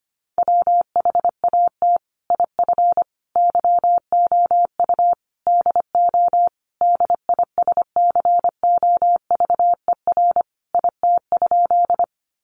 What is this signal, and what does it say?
Morse code, heard as a keyed tone, about 25 wpm, 700 Hz. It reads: WHAT IF YOU DO DISCOVER IT?
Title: Sentences from Top 1100 Words 25wpm